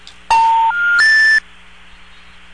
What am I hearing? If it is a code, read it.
Special information tone.